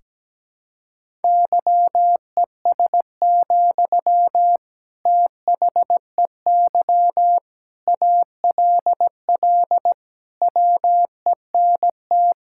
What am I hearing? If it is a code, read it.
Morse code, heard as a keyed tone, about 17 wpm, 700 Hz. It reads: YES, THEY ALL WENT
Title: Sentences from Top 1600 Words 17wpm